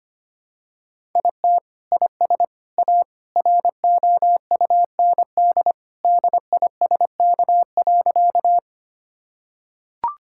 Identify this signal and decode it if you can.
Morse code, heard as a keyed tone, about 25 wpm, 700 Hz. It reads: IT IS A ROUND DISK.
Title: Sentences from Top 1000 Words - Review Part 6 - 25wpm